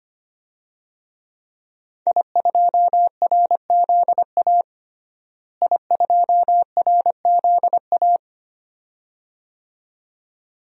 Morse code, heard as a keyed tone, about 25 wpm, 700 Hz. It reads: I2RZA I2RZA
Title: DXCC-100 Call Signs - 5 Characters - Repeated 2x - 25wpm